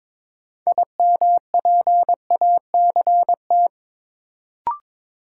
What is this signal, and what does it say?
Morse code, heard as a keyed tone, about 22 wpm, 700 Hz. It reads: IMPACT
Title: Sentences from Top 1000 Words - Review Part 6 - New Word Drill - 22wpm